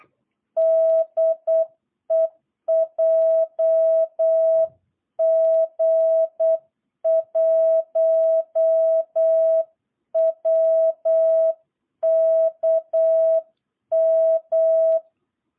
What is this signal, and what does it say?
Morse code, heard as a keyed tone, about 8 wpm, 660 Hz. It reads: DE JG1WKM
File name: jh8cbh / 2022cw